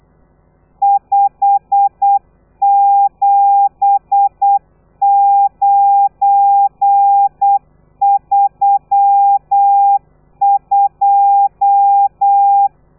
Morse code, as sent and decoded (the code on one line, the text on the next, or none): ..... --... ----. ...-- ..---
57932